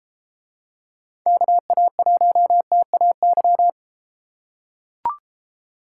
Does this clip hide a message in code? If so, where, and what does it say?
Yes: Morse code from 1.3 to 3.7 s, heard as a keyed tone, about 33 wpm, 700 Hz. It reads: KA1TAY